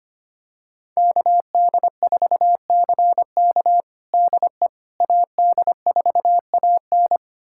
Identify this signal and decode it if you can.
Morse code, heard as a keyed tone, about 25 wpm, 700 Hz. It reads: KD4CK DE AD4AN
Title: USA Callsign DE Callsign - 5 Characters - 25wpm